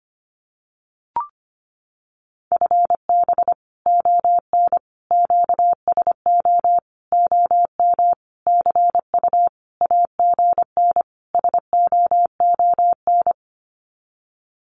Morse code, heard as a keyed tone, about 25 wpm, 700 Hz. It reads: FB ON QSO OM CU AGN SOON
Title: Common QSO Phrases 25wpm